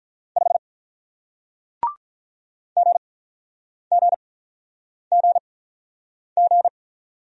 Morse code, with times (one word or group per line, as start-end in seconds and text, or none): 0.4-0.6 5
2.8-6.7 GGGG